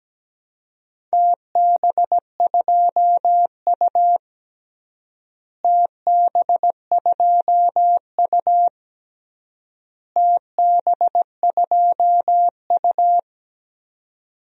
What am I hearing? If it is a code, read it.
Morse code, heard as a keyed tone, about 17 wpm, 700 Hz. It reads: TB2U TB2U TB2U
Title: DXCC-100 Call Signs - 4 Characters - Repeated 3x - 17wpm